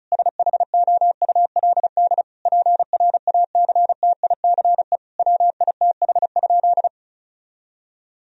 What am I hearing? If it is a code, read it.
Morse code, heard as a keyed tone, about 36 wpm, 700 Hz. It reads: SHOULD PRACTICE WITH?
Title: Sentences from Top 1000 Words - Review Part 4 - 35wpm